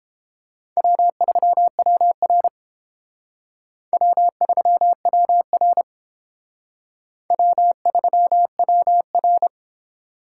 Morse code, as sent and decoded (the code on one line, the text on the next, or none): .-- ...-- .-- .-. / .-- ...-- .-- .-. / .-- ...-- .-- .-.
W3WR W3WR W3WR